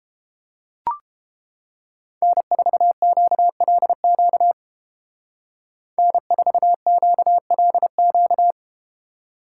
Morse code, heard as a keyed tone, about 32 wpm, 700 Hz. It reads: N4QLQ N4QLQ